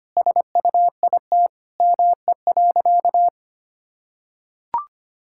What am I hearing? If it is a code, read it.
Morse code, heard as a keyed tone, about 25 wpm, 700 Hz. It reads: SUIT ME.
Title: Sentences from Top 1600 Words 25wpm